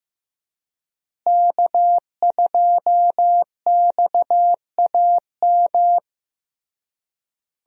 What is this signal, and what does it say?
Morse code, heard as a keyed tone, about 15 wpm, 700 Hz. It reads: K2XAM
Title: USA Call Signs - 5 Characters - Repeated 3x - 15wpm